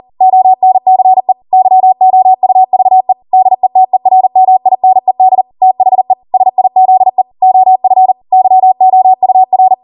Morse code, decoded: ONCE YOUVE DETERMINED THE SIZE OF YOUR